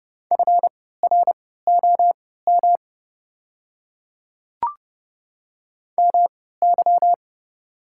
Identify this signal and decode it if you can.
Morse code, heard as a keyed tone, about 30 wpm, spaced as for 17 wpm, 700 Hz. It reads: FROM MY
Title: 3x Character Spacing - Top 100 Words - 30wpm